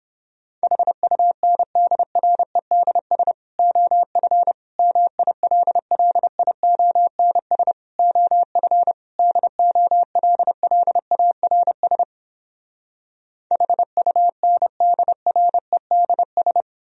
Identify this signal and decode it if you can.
Morse code, heard as a keyed tone, about 31 wpm, 700 Hz. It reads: HUNDREDS OF MILLIONS OF DOLLARS HUNDREDS